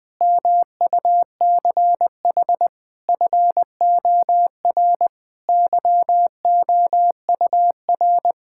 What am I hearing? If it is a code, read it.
Morse code, heard as a keyed tone, about 20 wpm, 700 Hz. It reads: MUCH FOR YOUR
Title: Sentences with 4-Word Phrases - 20wpm